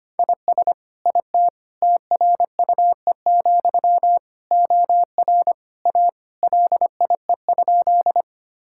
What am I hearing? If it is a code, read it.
Morse code, heard as a keyed tone, about 25 wpm, 700 Hz. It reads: IS IT TRUE, OR A LIE?
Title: Sentences from Top 1600 Words 25wpm